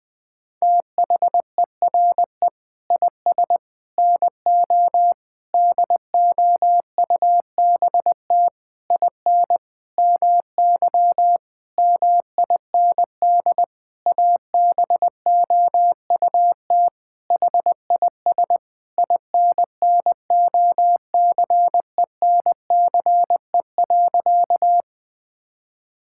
Morse code, heard as a keyed tone, about 20 wpm, 700 Hz. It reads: THERE IS NO DOUBT IN MY MIND ABOUT HIS INNOCENCE.
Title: Sentences with 7-Word Phrases - Spoken Beginning - 20wpm